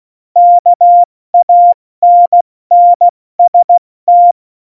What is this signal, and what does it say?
Morse code, heard as a keyed tone, about 16 wpm, 700 Hz. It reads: KANNST
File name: Morsekurs / Witze / Witz 27 Juni 17, Tempo 70 F80